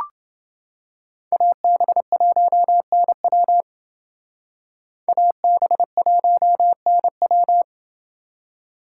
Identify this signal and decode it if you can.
Morse code, heard as a keyed tone, about 28 wpm, 700 Hz. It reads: AB1NW AB1NW